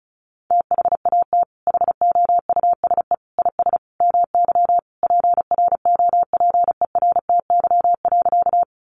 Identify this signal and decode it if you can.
Morse code, heard as a keyed tone, about 34 wpm, 700 Hz. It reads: THAT HOUSE IS MY PROPERTY.
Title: Sentences from Top 1000 Words 35wpm